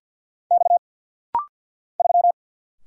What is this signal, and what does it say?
Morse code, heard as a keyed tone, about 46 wpm, 700 Hz. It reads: X 3
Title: Single Letter-Number Alt - Rapid-Fire 50wpm